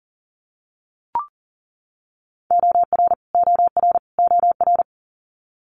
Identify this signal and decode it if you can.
Morse code, heard as a keyed tone, about 38 wpm, 700 Hz. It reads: OR OR OR
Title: Multi-letter Phonemes Repeated 3x 40wpm